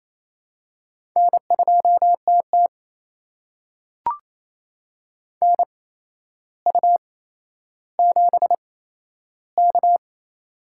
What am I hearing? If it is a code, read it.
Morse code, heard as a keyed tone, about 28 wpm, spaced as for 12 wpm, 700 Hz. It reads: N2TT NU7K
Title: USA Call Signs - 8x Character Spacing - 28wpm